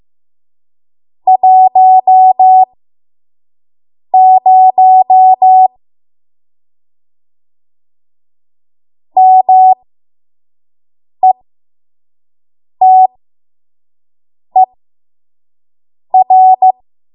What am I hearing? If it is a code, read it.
Morse code, heard as a keyed tone, about 15 wpm, spaced as for 5 wpm, 750 Hz. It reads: 10 METER
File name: Morse / 05WPM